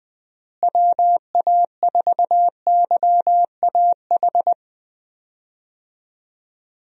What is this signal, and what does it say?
Morse code, heard as a keyed tone, about 20 wpm, 700 Hz. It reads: WA4YAH
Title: USA Call Signs - 6 Characters - Repeated 2x - 20wpm